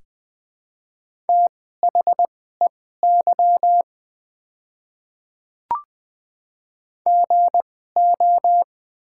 Morse code, heard as a keyed tone, about 20 wpm, spaced as for 14 wpm, 700 Hz. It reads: THEY GO
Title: 2x Character Spacing - Top 200 Words - 20wpm